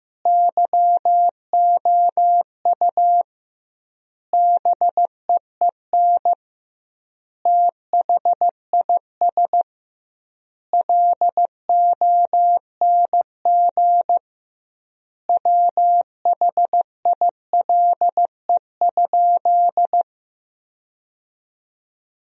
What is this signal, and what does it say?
Morse code, heard as a keyed tone, about 15 wpm, 700 Hz. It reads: YOU BEEN THIS LONG WHILE?
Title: 2x Word-Spacing - Sentences from Top 300 Words - 15wpm